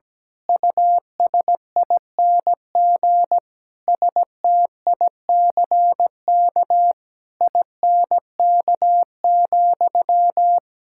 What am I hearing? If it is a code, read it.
Morse code, heard as a keyed tone, about 17 wpm, 700 Hz. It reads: USING STICK INK,
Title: Pangram Challenge v2 - 17wpm